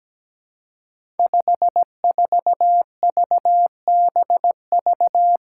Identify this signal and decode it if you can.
Morse code, heard as a keyed tone, about 17 wpm, 700 Hz. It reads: 54VBV
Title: Five Challenging Letter-Number Combinations 17wpm